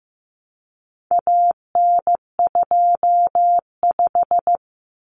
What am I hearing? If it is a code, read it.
Morse code, heard as a keyed tone, about 15 wpm, 700 Hz. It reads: AN25
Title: Common Cut Numbers - 4-Digit 15wpm